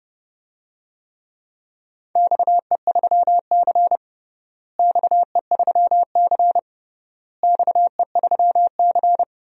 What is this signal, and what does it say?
Morse code, heard as a keyed tone, about 30 wpm, 700 Hz. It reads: XE3C XE3C XE3C
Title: DXCC-100 Call Signs - 4 Characters - Repeated 3x - 30wpm